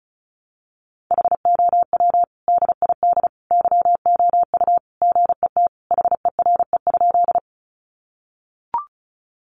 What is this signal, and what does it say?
Morse code, heard as a keyed tone, about 34 wpm, 700 Hz. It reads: HOW DID YOU GET HERE?
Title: Sentences from Top 200 Words 35wpm